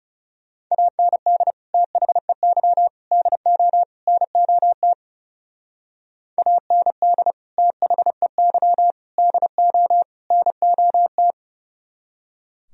Morse code, heard as a keyed tone, about 36 wpm, 700 Hz. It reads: AND THEY DO NOT AND THEY DO NOT
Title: Speed-Racing - Sets of 4 Words 30wpm